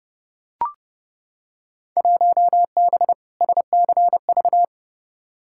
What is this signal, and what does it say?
Morse code, heard as a keyed tone, about 30 wpm, 700 Hz. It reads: 1B SCV
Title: Field Day Exchanges Encore - 30wpm